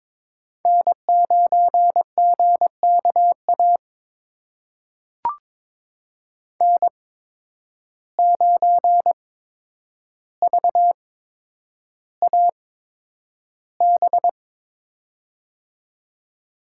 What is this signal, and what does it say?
Morse code, heard as a keyed tone, about 23 wpm, spaced as for 10 wpm, 700 Hz. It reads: N9GKA N9VAB
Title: USA Call Signs - 5 Characters - 8x Character Spacing - 22wpm